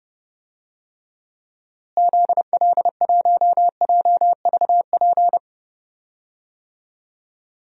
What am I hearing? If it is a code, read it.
Morse code, heard as a keyed tone, about 30 wpm, 700 Hz. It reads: ZL1JVP
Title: DXCC-100 Call Signs - 6 Characters - Repeated 3x - 30wpm